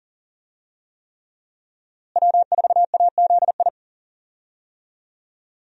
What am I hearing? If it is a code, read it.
Morse code, heard as a keyed tone, about 37 wpm, 700 Hz. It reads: W4AZI
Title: USA Call Signs - 5 Characters - Repeated 3x - 40wpm